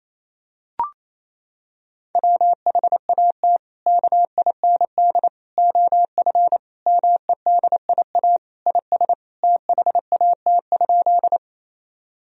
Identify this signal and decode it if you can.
Morse code, heard as a keyed tone, about 28 wpm, 700 Hz. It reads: WHAT KIND OF MEDIA IS THAT?
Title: Sentences from Top 1000 Words - Review Part 4 - 28wpm